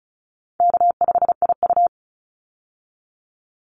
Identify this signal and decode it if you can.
Morse code, heard as a keyed tone, about 34 wpm, 700 Hz. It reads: K5IU
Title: Easy Call Signs Repeated 2x - 35wpm